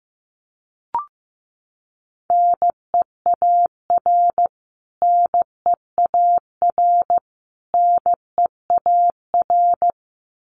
Morse code, heard as a keyed tone, about 15 wpm, 700 Hz. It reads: NEAR NEAR NEAR
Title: Top 500 Words Repeated 15wpm